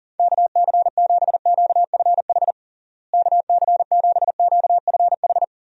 Morse code, heard as a keyed tone, about 40 wpm, 700 Hz. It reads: KC7QFH KC7QFH
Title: USA Call Signs - 6 Characters - Repeated 3x - 40wpm